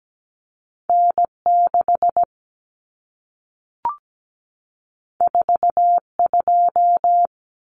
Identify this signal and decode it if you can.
Morse code, heard as a keyed tone, about 17 wpm, 700 Hz. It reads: N6 42
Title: Common Cut Numbers - 2-Digit 17wpm